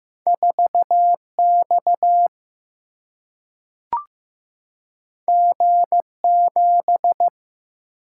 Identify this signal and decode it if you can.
Morse code, heard as a keyed tone, about 15 wpm, 700 Hz. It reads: 4X G7
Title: Single-Letter Single-Number 15wpm